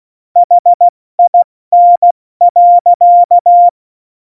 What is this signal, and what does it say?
Morse code, heard as a keyed tone, about 16 wpm, 700 Hz. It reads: HIN.
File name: Morsekurs / Witze / Witz 08. Dezember 2020, 70 F80